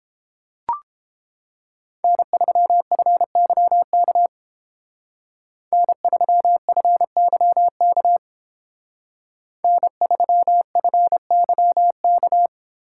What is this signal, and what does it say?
Morse code, heard as a keyed tone, about 30 wpm, 700 Hz. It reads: N3FYK N3FYK N3FYK